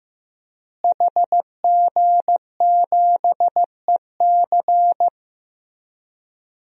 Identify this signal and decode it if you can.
Morse code, heard as a keyed tone, about 15 wpm, 700 Hz. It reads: HG7EC
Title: DXCC-100 Call Signs - 5 Characters - Repeated 3x - 15wpm